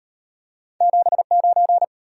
Morse code, heard as a keyed tone, about 37 wpm, 700 Hz. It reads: 79